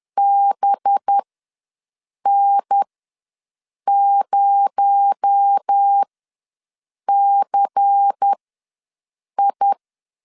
Morse code, heard as a keyed tone, about 10 wpm, spaced as for 6 wpm, 800 Hz. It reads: BN0CI
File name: LEZIONI / 9 2 ws 10wpm 4130